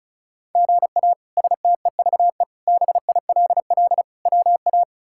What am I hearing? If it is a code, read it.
Morse code, heard as a keyed tone, about 36 wpm, 700 Hz. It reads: GA STEVE BILL WA
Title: K1USN SST QSO Practice - 35wpm